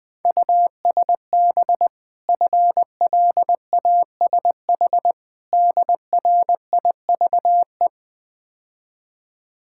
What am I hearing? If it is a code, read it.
Morse code, heard as a keyed tone, about 22 wpm, 700 Hz. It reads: USB FLASH DRIVE
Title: Easy Words - Speed-Racing with Hints - 20wpm